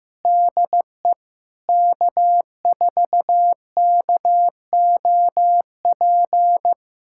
Morse code, heard as a keyed tone, about 15 wpm, 700 Hz. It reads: DE K4KOP
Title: USA Callsign DE Callsign - 5 Characters - 15wpm